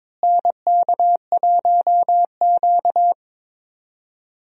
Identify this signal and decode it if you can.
Morse code, heard as a keyed tone, about 22 wpm, 700 Hz. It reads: NK1Q
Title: USA Call Signs - 4 Characters - Repeated 3x - 22wpm